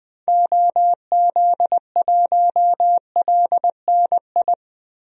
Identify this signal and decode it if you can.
Morse code, heard as a keyed tone, about 20 wpm, 700 Hz. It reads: OZ1LNI
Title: DXCC-100 Call Signs - 6 Characters - Repeated 3x - 20wpm